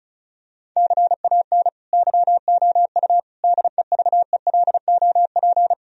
Morse code, heard as a keyed tone, about 36 wpm, 700 Hz. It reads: CAN YOU DEVELOP
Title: Sentences from Top 1000 Words - Review Part 2 - 35wpm